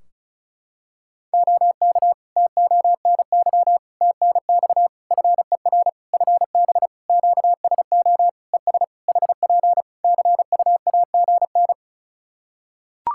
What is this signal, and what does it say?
Morse code, heard as a keyed tone, about 34 wpm, 700 Hz. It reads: OK TONY TNX FER FB QSO ES HP CUAGN
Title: Practice QSO - Exchange 4 - Ending - 35wpm